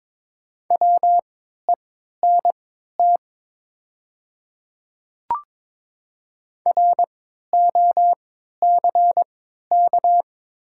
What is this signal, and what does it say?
Morse code, heard as a keyed tone, about 22 wpm, spaced as for 12 wpm, 690 Hz. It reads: WENT ROCK
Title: 3x Character Spacing - Top 500 Words - 22wpm